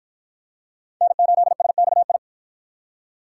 Morse code, heard as a keyed tone, about 53 wpm, 700 Hz. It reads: N8SXI